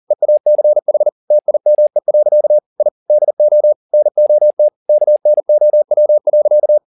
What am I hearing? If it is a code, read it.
Morse code, heard as a keyed tone, about 40 wpm, 600 Hz. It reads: EACH TIME. I DO NOT KNOW.
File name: the old man and the sea 2 40wpm